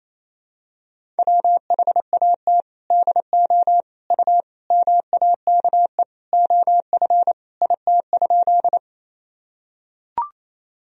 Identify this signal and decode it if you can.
Morse code, heard as a keyed tone, about 27 wpm, 700 Hz. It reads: WHAT DO U MAKE OF IT?
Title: Sentences from Top 100 Words and Abbreviations 28wpm